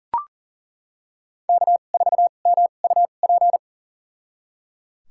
Morse code, heard as a keyed tone, about 37 wpm, spaced as for 28 wpm, 700 Hz. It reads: K4MUP
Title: USA Call Signs - 2x Character Spacing - 40wpm